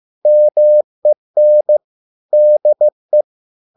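Morse code, heard as a keyed tone, about 15 wpm, 600 Hz. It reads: MEN DE
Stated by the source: Morse code at a slow 15 wpm from CW med Gnister.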